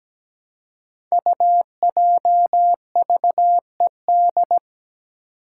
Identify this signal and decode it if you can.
Morse code, heard as a keyed tone, about 17 wpm, 700 Hz. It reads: UJVED
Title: Five Challenging Letter-Number Combinations 17wpm